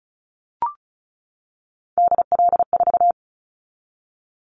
Morse code, heard as a keyed tone, about 36 wpm, 700 Hz. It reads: DL4
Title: Callsign Prefixes for Top 50 DXCC - 35wpm